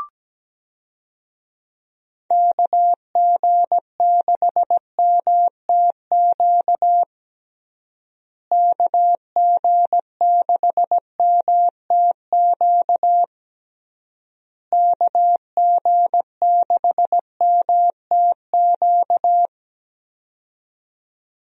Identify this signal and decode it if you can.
Morse code, heard as a keyed tone, about 17 wpm, 700 Hz. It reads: KG6MTQ KG6MTQ KG6MTQ
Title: USA Call Signs - 6 Characters - Repeated 3x - 17wpm